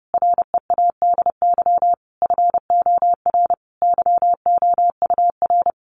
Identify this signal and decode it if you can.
Morse code, heard as a keyed tone, about 30 wpm, 700 Hz. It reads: READY FOR YOUR
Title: Sentences from Top 800 Words 30wpm